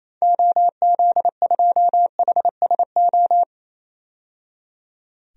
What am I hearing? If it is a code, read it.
Morse code, heard as a keyed tone, about 28 wpm, 700 Hz. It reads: OZ2HSO
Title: DXCC-100 Call Signs - 6 Characters - 28wpm